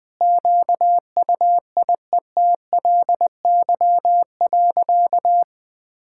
Morse code, heard as a keyed tone, about 20 wpm, 700 Hz. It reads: QUIETLY.
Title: Sentences with 2-Word Phrases - 20wpm